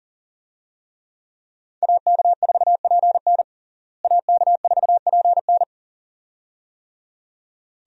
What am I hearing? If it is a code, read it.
Morse code, heard as a keyed tone, about 37 wpm, 700 Hz. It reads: AK4PN AK4PN
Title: USA Call Signs - 5 Characters - Repeated 2x - 40wpm